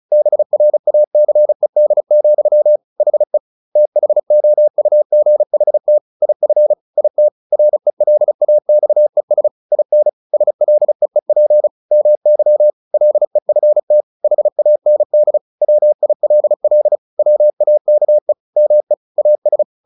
Morse code, heard as a keyed tone, about 34 wpm, 600 Hz. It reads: BRACED, HE THOUGHT IF IT RELAXES IN SLEEP MY LEFT HAND WILL WAKE ME AS
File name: the old man and the sea 2 35wpm